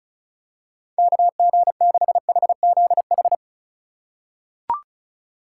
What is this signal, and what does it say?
Morse code, heard as a keyed tone, about 34 wpm, 690 Hz. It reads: KG6HZH
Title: USA Call Signs - 6 Characters - 3x Character Spacing - 35wpm